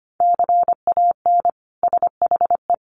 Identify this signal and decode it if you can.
Morse code, heard as a keyed tone, about 25 wpm, 700 Hz. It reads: CAN SHE
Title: Sentences from Top 1100 Words 25wpm